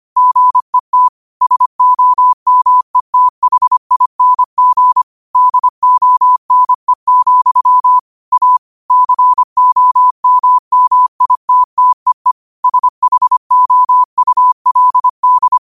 Morse code, heard as a keyed tone, about 25 wpm, 1000 Hz. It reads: GET SOMETHING DONE, A COMMITTEE SHOULD
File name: qotdfiles / 25wpm